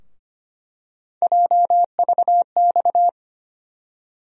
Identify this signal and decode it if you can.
Morse code, heard as a keyed tone, about 25 wpm, 700 Hz. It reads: JVX
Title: USA Call Sign Postfixes - 3 Letters 25wpm